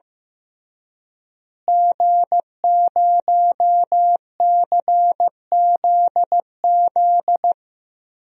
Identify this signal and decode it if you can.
Morse code, heard as a keyed tone, about 15 wpm, 700 Hz. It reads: G0CZZ
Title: DXCC-100 Call Signs - 5 Characters - Repeated 3x - 15wpm